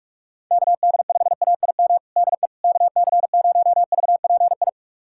Morse code, heard as a keyed tone, about 41 wpm, 700 Hz. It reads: KD5AIM DE KC0VPI
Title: USA Callsign DE Callsign 45wpm